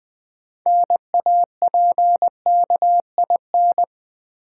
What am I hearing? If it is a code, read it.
Morse code, heard as a keyed tone, about 20 wpm, 700 Hz. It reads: NAPKIN
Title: Easy Words with Hints - 20wpm